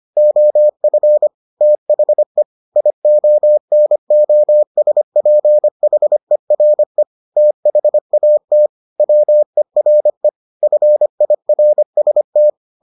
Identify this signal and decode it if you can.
Morse code, heard as a keyed tone, about 25 wpm, 600 Hz. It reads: OF THE IONOSPHERE THAT WERE FIRST